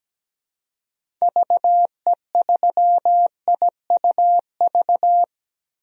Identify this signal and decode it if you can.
Morse code, heard as a keyed tone, about 17 wpm, 700 Hz. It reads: VE3IUV
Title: DXCC-100 Call Signs - 6 Characters - 17wpm